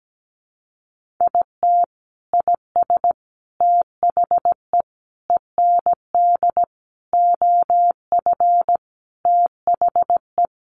Morse code, heard as a keyed tone, about 17 wpm, 700 Hz. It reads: IT IS THE END OF THE
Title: Sentences from Top 1500 Words 17wpm